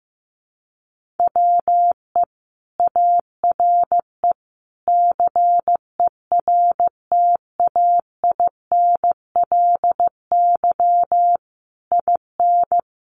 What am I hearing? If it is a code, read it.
Morse code, heard as a keyed tone, about 15 wpm, 700 Hz. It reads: WE ARE CERTAINLY IN
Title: Sentences from Top 1500 Words 15wpm